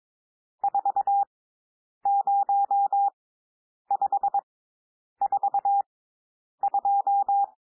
Morse code, heard as a keyed tone, about 22 wpm, spaced as for 9 wpm, 800 Hz. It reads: 40542
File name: cwmp3 / N1 SOUND NUMBERS 22 wpm